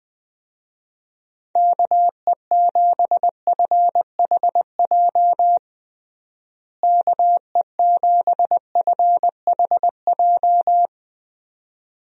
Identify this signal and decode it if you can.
Morse code, heard as a keyed tone, about 20 wpm, 700 Hz. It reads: KE7FHJ KE7FHJ
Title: USA Call Signs - 6 Characters - Repeated 2x - 20wpm